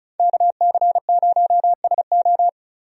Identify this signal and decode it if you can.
Morse code, heard as a keyed tone, about 34 wpm, 700 Hz. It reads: KC0SO
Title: USA Call Signs - 5 Characters - Repeated 2x - 35wpm